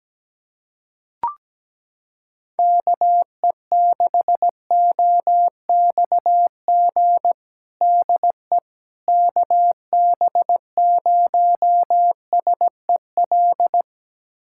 Morse code, heard as a keyed tone, about 17 wpm, 700 Hz. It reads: KE6OXG DE KB0SEL
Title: USA Callsign DE Callsign - 6 Characters - 17wpm